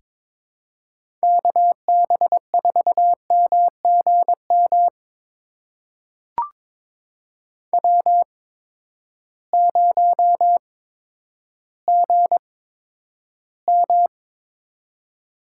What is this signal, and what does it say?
Morse code, heard as a keyed tone, about 22 wpm, spaced as for 11 wpm, 700 Hz. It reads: KB4MGM W0GM
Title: USA Call Signs - 8x Character Spacing - 22wpm